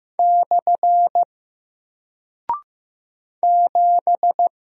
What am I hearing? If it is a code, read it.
Morse code, heard as a keyed tone, about 15 wpm, 700 Hz. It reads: / 7
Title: Single Character (Letter:Number:Punctuation:Prosign) Alt 15wpm